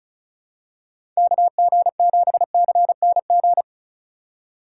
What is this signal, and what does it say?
Morse code, heard as a keyed tone, about 36 wpm, 700 Hz. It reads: KG7CNG
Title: USA Call Signs - 6 Characters - 2x Character Spacing - 35wpm